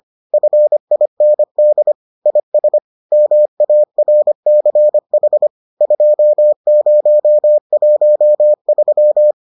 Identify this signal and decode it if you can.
Morse code, heard as a keyed tone, about 25 wpm, 600 Hz. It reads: FIND IS MARCH 2013